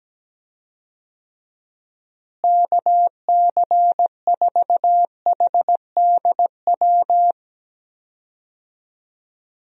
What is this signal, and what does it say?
Morse code, heard as a keyed tone, about 17 wpm, 700 Hz. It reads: KC4HDW
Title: USA Call Signs - 6 Characters - Repeated 2x - 17wpm